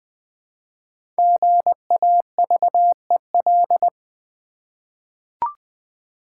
Morse code, heard as a keyed tone, about 20 wpm, 700 Hz. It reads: GAVEL
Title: Easy Words with Hints - 20wpm